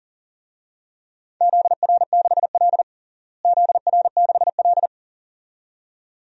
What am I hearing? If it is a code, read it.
Morse code, heard as a keyed tone, about 40 wpm, 700 Hz. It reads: ZR6L ZR6L
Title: DXCC-100 Call Signs - 4 Characters - Repeated 2x - 40wpm